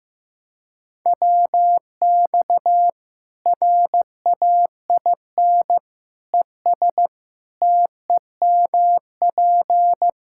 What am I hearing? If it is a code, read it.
Morse code, heard as a keyed tone, about 15 wpm, 700 Hz. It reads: WX RAIN ES TEMP
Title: Practice QSO - Exchange 2 - Info - 15wpm